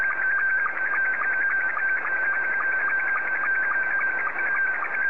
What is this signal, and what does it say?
SSTV (slow-scan television), Scottie 2.
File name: sstv / S2-JA7QJQ